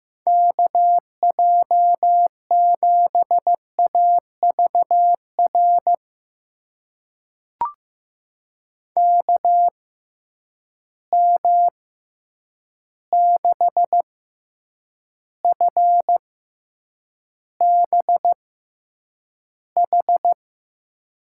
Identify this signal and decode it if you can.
Morse code, heard as a keyed tone, about 15 wpm, spaced as for 8 wpm, 700 Hz. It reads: KJ7AVR KM6FBH
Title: USA Call Signs - 6 Characters - 6x Character Spacing - 15wpm